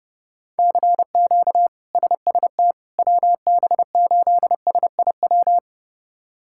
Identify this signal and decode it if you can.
Morse code, heard as a keyed tone, about 30 wpm, 700 Hz. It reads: CQ SST WB8SIW
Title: K1USN SST QSO Practice - 30wpm